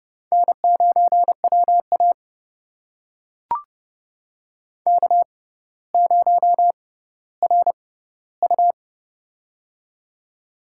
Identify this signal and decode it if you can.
Morse code, heard as a keyed tone, about 30 wpm, spaced as for 15 wpm, 690 Hz. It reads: N9WA K0RU
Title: USA Call Signs - 4 Characters - 6x Character Spacing - 30wpm